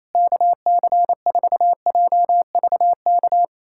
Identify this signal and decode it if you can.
Morse code, heard as a keyed tone, about 28 wpm, 700 Hz. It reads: KC4JVK
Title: USA Callsign DE Callsign - 6 Characters - 28wpm